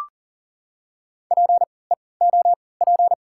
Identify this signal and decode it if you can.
Morse code, heard as a keyed tone, about 38 wpm, spaced as for 22 wpm, 690 Hz. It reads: PEOP
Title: 3x Character Spacing - Top 200 Words - 40wpm